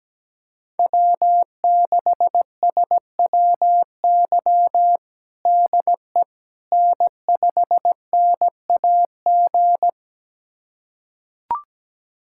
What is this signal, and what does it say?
Morse code, heard as a keyed tone, about 17 wpm, 700 Hz. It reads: W6SWY DE N5NAG
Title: USA Callsign DE Callsign 17wpm